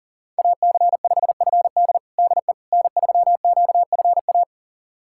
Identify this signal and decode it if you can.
Morse code, heard as a keyed tone, about 40 wpm, 700 Hz. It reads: AC5FD DE N3QFA
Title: USA Callsign DE Callsign - 5 Characters - 40wpm